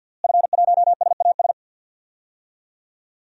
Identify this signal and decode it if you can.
Morse code, heard as a keyed tone, about 47 wpm, 700 Hz. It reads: F1SAS
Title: DXCC-100 Call Signs - 5 Characters - Repeated 3x - 50wpm